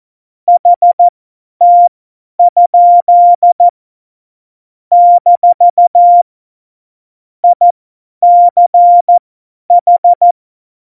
Morse code, heard as a keyed tone, about 14 wpm, spaced as for 10 wpm, 700 Hz. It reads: HT? - ICH
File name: Morsekurs / Witze / Witz 05. Januar 2021, 50 F70